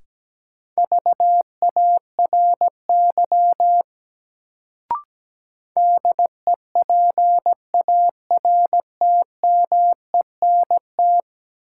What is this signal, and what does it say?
Morse code, heard as a keyed tone, about 17 wpm, 700 Hz. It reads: VARY DEPARTMENT
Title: Sentences from Top 1200 Words - New Word Drill Alt - 17wpm